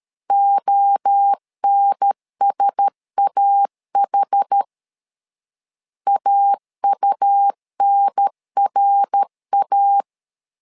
Morse code, heard as a keyed tone, about 13 wpm, 800 Hz. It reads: ONSAH AUNRA
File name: LEZIONI / 4 Q1S 12wpm 1802